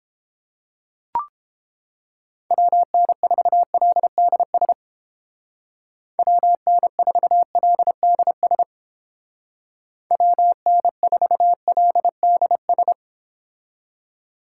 Morse code, heard as a keyed tone, about 29 wpm, 700 Hz. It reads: WN4LDS WN4LDS WN4LDS